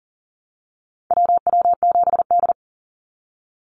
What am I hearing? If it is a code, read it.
Morse code, heard as a keyed tone, about 42 wpm, 700 Hz. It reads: WW7D
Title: NA SOTA Activator Callsigns 40wpm